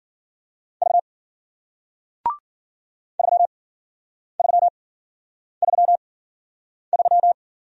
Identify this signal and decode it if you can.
Morse code, heard as a keyed tone, about 53 wpm, spaced as for 9 wpm, 700 Hz. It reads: V 3333